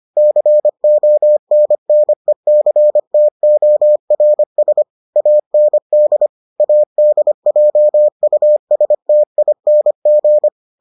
Morse code, heard as a keyed tone, about 25 wpm, 600 Hz. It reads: CONNECTORS AND ADJUSTING